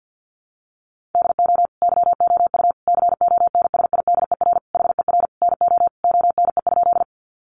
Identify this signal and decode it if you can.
Morse code, heard as a keyed tone, about 50 wpm, 700 Hz. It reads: DO YOU CONSIDER HER NO ONE?
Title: Sentences from Top 600 Words 50wpm